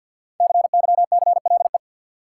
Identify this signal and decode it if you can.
Morse code, heard as a keyed tone, about 52 wpm, 700 Hz. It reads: CYCLE
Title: Sentences from Top 900 Words - New Word Drill - 50wpm